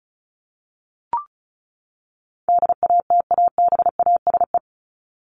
Morse code, heard as a keyed tone, about 34 wpm, 700 Hz. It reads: DATABASE
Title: Top 1751-2000 Words 35wpm